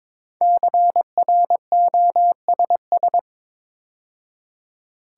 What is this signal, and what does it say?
Morse code, heard as a keyed tone, about 22 wpm, 700 Hz. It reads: CROSS
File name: 5 Letter Words - First 3 Letters Given - 22wpm